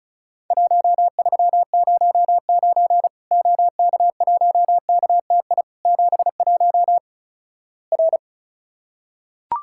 Morse code, heard as a keyed tone, about 36 wpm, 700 Hz. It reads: 1309 OK1KTI 71 R
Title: Worked All Europe DX Contest - QTC Exchange - 35wpm